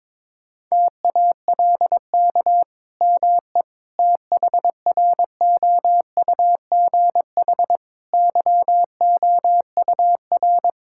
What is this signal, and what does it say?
Morse code, heard as a keyed tone, about 22 wpm, 700 Hz. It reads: TALK ME THROUGH YOUR
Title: Sentences from Top 1000 Words - Review Part 5 - 22wpm